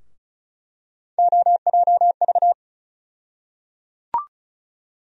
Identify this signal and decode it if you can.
Morse code, heard as a keyed tone, about 36 wpm, 700 Hz. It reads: OJV
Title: USA Call Sign Postfixes - 3 Letters 35wpm